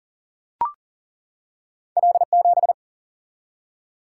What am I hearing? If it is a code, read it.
Morse code, heard as a keyed tone, about 40 wpm, 700 Hz. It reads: L7
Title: Single-Letter Single-Number 40wpm